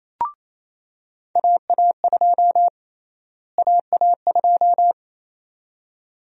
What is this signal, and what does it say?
Morse code, heard as a keyed tone, about 28 wpm, 700 Hz. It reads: AA2 AA2
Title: USA Call Sign Prefixes - Repeated 2x - 28wpm